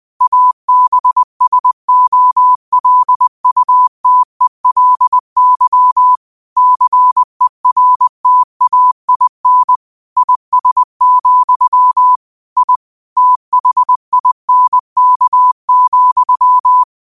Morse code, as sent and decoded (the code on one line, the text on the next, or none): .- -... ... --- .-.. ..- - . .-.. -.-- / -.-. .-. - .- .. -. / .. ... --..-- / .. / - .... .. -. -.- --..--
ABSOLUTELY CERTAIN IS, I THINK,